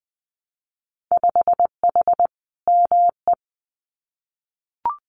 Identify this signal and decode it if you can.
Morse code, heard as a keyed tone, about 20 wpm, 700 Hz. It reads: 5H ME
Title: Winter Field Day Exchanges Encore 20wpm